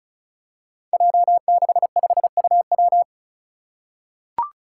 Morse code, heard as a keyed tone, about 34 wpm, 700 Hz. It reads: J65UW
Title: Five Challenging Letter-Number Combinations 35wpm